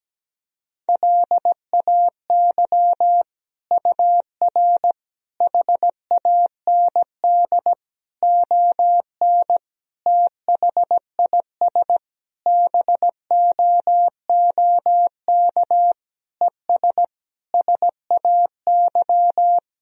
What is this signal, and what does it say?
Morse code, heard as a keyed tone, about 17 wpm, 700 Hz. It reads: LAY UR HAND ON THIS BOOK ES SAY
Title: Sentences from Top 400 Words and Abbreviations 17wpm